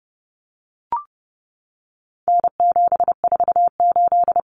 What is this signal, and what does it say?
Morse code, heard as a keyed tone, about 29 wpm, 700 Hz. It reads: N748
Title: Common Cut Numbers - 4-Digit 30wpm